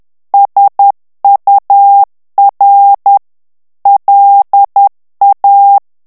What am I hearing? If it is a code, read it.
Morse code, heard as a keyed tone, about 10 wpm, 800 Hz. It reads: SUR LA
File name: morse mp3 / FRANCAIS TEXTE CLAIR 2 VITESSE 10 MOTS MINUTES